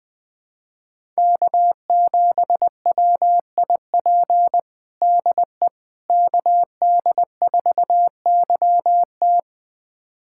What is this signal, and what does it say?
Morse code, heard as a keyed tone, about 21 wpm, 700 Hz. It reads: K7WIP DE KD4YT
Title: USA Callsign DE Callsign - 5 Characters - 20wpm